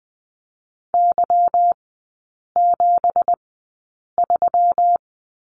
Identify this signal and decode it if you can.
Morse code, heard as a keyed tone, about 20 wpm, spaced as for 8 wpm, 700 Hz. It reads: Y73
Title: 166 Lesson 27e - Number 7 - Three Characters 20wpm